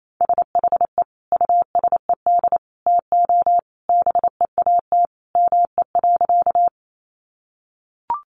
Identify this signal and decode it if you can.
Morse code, heard as a keyed tone, about 28 wpm, 700 Hz. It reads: SHE USED TO BEAT ME.
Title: Sentences from Top 1300 Words 28wpm